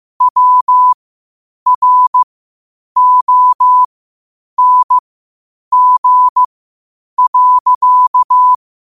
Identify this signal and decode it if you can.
Morse code, heard as a keyed tone, about 15 wpm, spaced as for 8 wpm, 1000 Hz. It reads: WRONG.
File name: qotdfiles / 8wpm